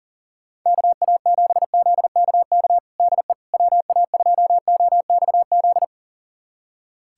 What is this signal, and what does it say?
Morse code, heard as a keyed tone, about 40 wpm, 700 Hz. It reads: KA7ZKK DE WA2OXZ
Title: USA Callsign DE Callsign - 6 Characters - 40wpm